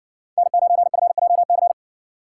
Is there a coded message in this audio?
Yes: Morse code from 0.4 to 1.7 s, heard as a keyed tone, about 60 wpm, 700 Hz. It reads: N9FJC